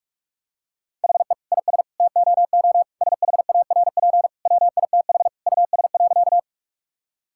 Morse code, heard as a keyed tone, about 42 wpm, 700 Hz. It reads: HE IS TOO SHARP WITH US.
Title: Sentences from Top 1600 Words 45wpm